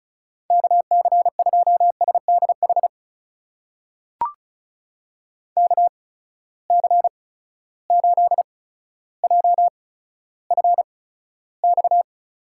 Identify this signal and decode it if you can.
Morse code, heard as a keyed tone, about 36 wpm, spaced as for 15 wpm, 700 Hz. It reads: KC2SDH KC8JFX
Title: USA Call Signs - 8x Character Spacing - 35wpm